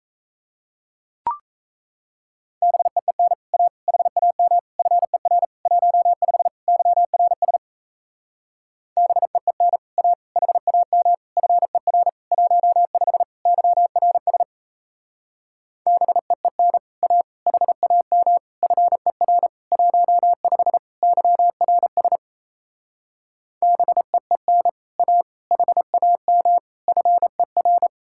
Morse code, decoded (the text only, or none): BEEN A HAM FER 15 YRS BEEN A HAM FER 15 YRS BEEN A HAM FER 15 YRS BEEN A HAM FER